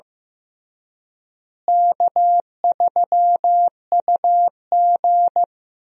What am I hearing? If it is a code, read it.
Morse code, heard as a keyed tone, about 16 wpm, 700 Hz. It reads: K3UG
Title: USA Call Signs - 4 Characters - Repeated 3x - 15wpm